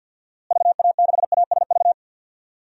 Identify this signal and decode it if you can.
Morse code, heard as a keyed tone, about 50 wpm, 700 Hz. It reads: VA6ASV
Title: DXCC-100 Call Signs - 6 Characters - Repeated 2x - 50wpm